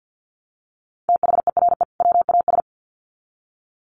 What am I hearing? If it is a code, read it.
Morse code, heard as a keyed tone, about 50 wpm, 700 Hz. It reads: THERE WAS
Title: Sets of 2 Words 50wpm